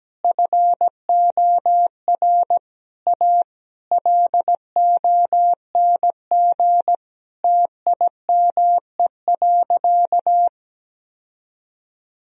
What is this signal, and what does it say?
Morse code, heard as a keyed tone, about 17 wpm, 700 Hz. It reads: FOR A LONG TIME.
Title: Sentences from Top 1000 Words - Review Part 1 - 17wpm